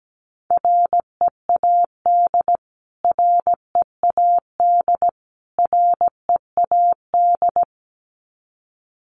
Morse code, heard as a keyed tone, about 17 wpm, 700 Hz. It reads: READ READ READ
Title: Top 200 Words Repeated 17wpm